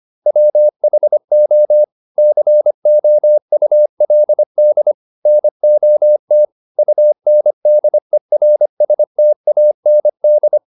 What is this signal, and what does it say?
Morse code, heard as a keyed tone, about 25 wpm, 600 Hz. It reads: WHO COULD NOT UNDERSTAND
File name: the old man and the sea part 1 25wpm